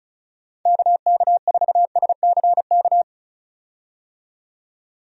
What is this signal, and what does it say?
Morse code, heard as a keyed tone, about 35 wpm, 700 Hz. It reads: KK4SCK
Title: USA Call Signs - 6 Characters - 35wpm